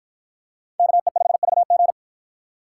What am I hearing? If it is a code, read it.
Morse code, heard as a keyed tone, about 48 wpm, 700 Hz. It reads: KE5VG